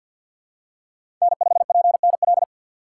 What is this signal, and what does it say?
Morse code, heard as a keyed tone, about 50 wpm, 700 Hz. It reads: N5PNL
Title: USA Call Signs - 4x Character Spacing - 50wpm